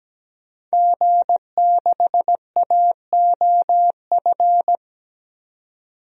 Morse code, heard as a keyed tone, about 17 wpm, 700 Hz. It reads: G6AOF
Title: DXCC-100 Call Signs - 5 Characters - 17wpm